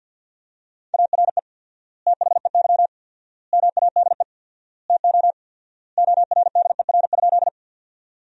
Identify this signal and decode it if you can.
Morse code, heard as a keyed tone, about 50 wpm, 700 Hz. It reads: ARE THEY MADE TO ORDER?
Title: 4x Word-Spacing - Sentences from Top 400 Words - 50wpm